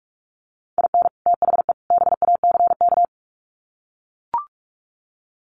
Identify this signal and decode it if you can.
Morse code, heard as a keyed tone, about 44 wpm, 700 Hz. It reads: IN THE BACK
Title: Sets of 3 Words 45wpm